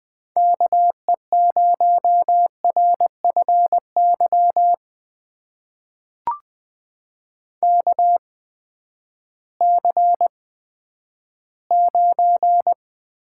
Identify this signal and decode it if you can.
Morse code, heard as a keyed tone, about 20 wpm, spaced as for 11 wpm, 700 Hz. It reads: KE0RFY KC9
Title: USA Call Signs - 6 Characters - 8x Character Spacing - 20wpm